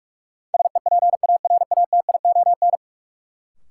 Morse code, heard as a keyed tone, about 48 wpm, 700 Hz. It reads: SEPARATION
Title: Sets of 5 Words - New Word Drill - 45wpm